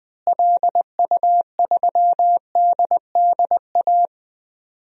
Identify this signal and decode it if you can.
Morse code, heard as a keyed tone, about 20 wpm, 700 Hz. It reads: LU3DDA
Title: DXCC-100 Call Signs - 6 Characters - Repeated 2x - 20wpm